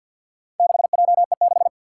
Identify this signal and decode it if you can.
Morse code, heard as a keyed tone, about 53 wpm, 700 Hz. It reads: 6JE6
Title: Four Challenging Letter-Number Combinations 50wpm